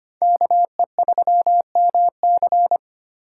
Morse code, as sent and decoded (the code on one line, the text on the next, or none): -.- . ...-- -- -.-.
KE3MC